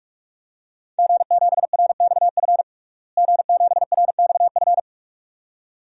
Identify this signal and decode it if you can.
Morse code, heard as a keyed tone, about 46 wpm, 700 Hz. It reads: G7RXF G7RXF
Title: DXCC-100 Call Signs - 5 Characters - Repeated 2x - 45wpm